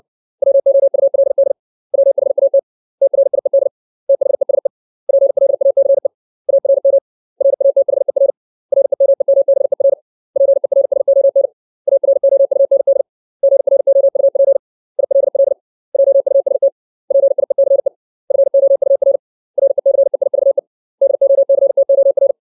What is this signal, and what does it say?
Morse code, heard as a keyed tone, about 60 wpm, 550 Hz. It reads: WOULD WHAT TRIED THESE PLACE ARM RATHER REMEMBER PERSON AROUND GROUP ILL JUST PIECE FOUR RECEIVE DOCTOR
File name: 632 words / 632 Words 60WPM